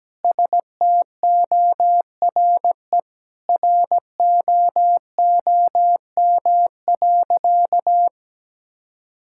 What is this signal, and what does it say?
Morse code, heard as a keyed tone, about 17 wpm, 700 Hz. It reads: STORE ROOM.
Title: Sentences from Top 1000 Words - Review Part 5 - 17wpm